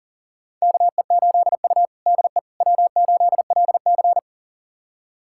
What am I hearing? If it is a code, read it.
Morse code, heard as a keyed tone, about 37 wpm, 700 Hz. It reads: KE8U DE W8LC
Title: USA Callsign DE Callsign - 4 Characters - 40wpm